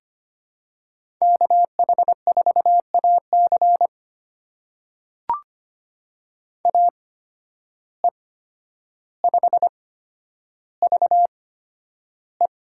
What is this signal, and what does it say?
Morse code, heard as a keyed tone, about 25 wpm, spaced as for 11 wpm, 700 Hz. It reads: KH4AC AE5VE
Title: USA Call Signs - 5 Characters - 8x Character Spacing - 25wpm